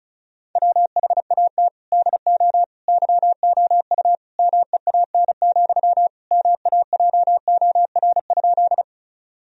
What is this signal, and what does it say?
Morse code, heard as a keyed tone, about 36 wpm, 700 Hz. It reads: WHAT DO YOU MEAN, MAJOR?
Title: Sentences from Top 1000 Words - Review Part 4 - 35wpm